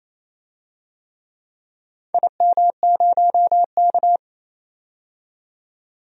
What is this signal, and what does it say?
Morse code, heard as a keyed tone, about 28 wpm, 700 Hz. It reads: IM0K
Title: DXCC-100 Call Signs - 4 Characters - 28wpm